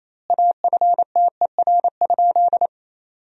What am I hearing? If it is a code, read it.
Morse code, heard as a keyed tone, about 28 wpm, 700 Hz. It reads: AFTER?
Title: 2x Word-Spacing - Sentences from Top 200 Words - 28wpm